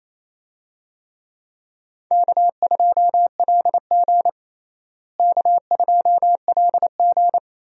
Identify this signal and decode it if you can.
Morse code, heard as a keyed tone, about 28 wpm, 700 Hz. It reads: K2LG K2LG
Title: USA Call Signs - 4 Characters - Repeated 2x - 28wpm